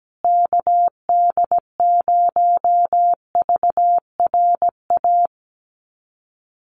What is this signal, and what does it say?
Morse code, heard as a keyed tone, about 17 wpm, 700 Hz. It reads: KD0VRA
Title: USA Call Signs Repeated 2x 17wpm